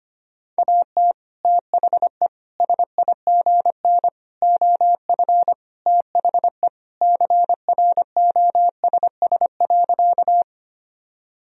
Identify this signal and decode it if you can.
Morse code, heard as a keyed tone, about 25 wpm, 700 Hz. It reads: AT THE SIGN OF THE CROSS.
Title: Sentences from Top 1000 Words - Review Part 1 - 25wpm